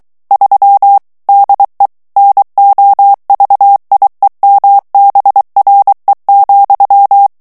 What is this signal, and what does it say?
Morse code, heard as a keyed tone, about 23 wpm, 800 Hz. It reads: 3 DE NOVIEMBRE,
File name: morse mp3 / ESPAGNOL TEXTE EN CLAIR 2 VITESSE 22 MOTS MINUTES